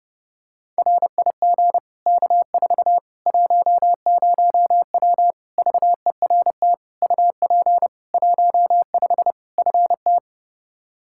Morse code, heard as a keyed tone, about 32 wpm, 700 Hz. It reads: RIG K4 10W VERT UP 15 FT
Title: Practice QSO - Exchange 2 - Info - 30wpm